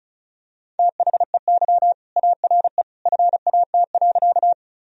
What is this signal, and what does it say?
Morse code, heard as a keyed tone, about 36 wpm, 700 Hz. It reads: THEY ARE FAT.
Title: Sentences from Top 1000 Words - Review Part 4 - 35wpm